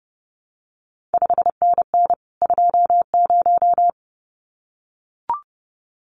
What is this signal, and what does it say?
Morse code, heard as a keyed tone, about 30 wpm, 700 Hz. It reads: 5NN 20
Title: CQ WW DX Contest Exchange Encore - 30wpm